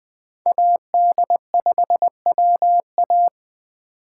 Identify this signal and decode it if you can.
Morse code, heard as a keyed tone, about 20 wpm, 700 Hz. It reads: AD5WA
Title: Speed Racing - USA 2x2 Call Signs 17wpm